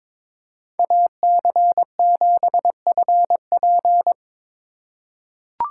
Morse code, heard as a keyed tone, about 22 wpm, 700 Hz. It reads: AC7FP
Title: USA Call Signs - 5 Characters - 4x Character Spacing - 22wpm